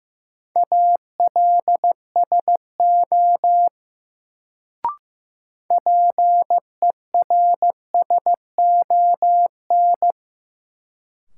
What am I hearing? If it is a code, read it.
Morse code, heard as a keyed tone, about 15 wpm, 700 Hz. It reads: ALSO PERSON
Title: Sentences from Top 400 Words - New Word Drill - 15wpm